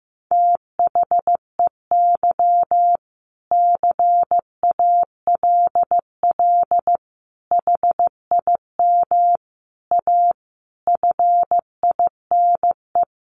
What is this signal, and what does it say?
Morse code, heard as a keyed tone, about 15 wpm, 700 Hz. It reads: THEY CALL HIM A FINE
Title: Sentences from Top 500 Words 15wpm